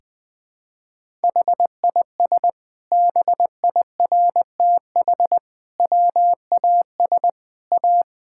Morse code, heard as a keyed tone, about 21 wpm, 700 Hz. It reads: HIS BIRTH WAS A
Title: Sentences from Top 1000 Words - Review Part 5 - 20wpm